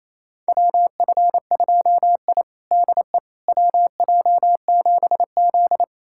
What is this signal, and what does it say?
Morse code, heard as a keyed tone, about 28 wpm, 700 Hz. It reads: WF2I DE WJ7Z
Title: USA Callsign DE Callsign - 4 Characters - 28wpm